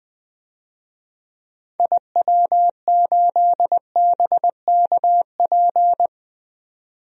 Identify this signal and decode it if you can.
Morse code, heard as a keyed tone, about 20 wpm, 700 Hz. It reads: IW8BKP
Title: DXCC-100 Call Signs - 6 Characters - 20wpm